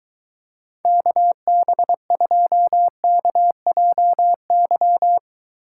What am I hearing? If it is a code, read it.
Morse code, heard as a keyed tone, about 23 wpm, 700 Hz. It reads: KB2KJY